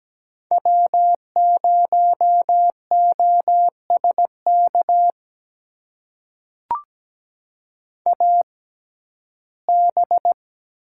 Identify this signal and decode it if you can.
Morse code, heard as a keyed tone, about 17 wpm, spaced as for 12 wpm, 700 Hz. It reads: W0OSK AB
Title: USA Call Signs - 5 Characters - 6x Character Spacing - 17wpm